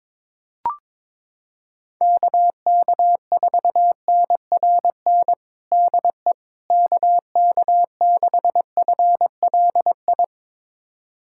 Morse code, heard as a keyed tone, about 21 wpm, 700 Hz. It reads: KK4NRN DE KK6FLI
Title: USA Callsign DE Callsign - 6 Characters - 22wpm